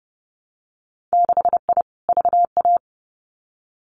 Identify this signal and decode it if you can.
Morse code, heard as a keyed tone, about 30 wpm, 700 Hz. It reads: 6I VA
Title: Winter Field Day Exchanges Encore 30wpm